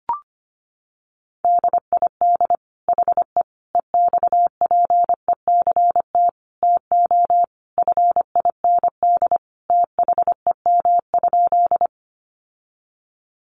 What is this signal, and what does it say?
Morse code, heard as a keyed tone, about 25 wpm, 700 Hz. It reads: DID HE EXPECT TO FIND THEM?
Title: Sentences from Top 1100 Words 25wpm